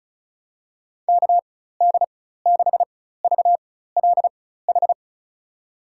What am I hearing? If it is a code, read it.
Morse code, heard as a keyed tone, about 36 wpm, spaced as for 16 wpm, 700 Hz. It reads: KD6VLH
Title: USA Call Signs - 6 Characters - 4x Character Spacing - 35wpm